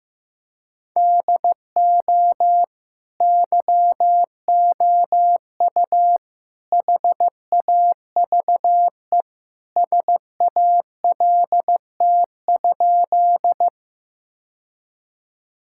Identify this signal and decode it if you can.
Morse code, heard as a keyed tone, about 16 wpm, 700 Hz. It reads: DO YOU HAVE SALT?
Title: Sentences from Top 1000 Words - Review Part 5 - 15wpm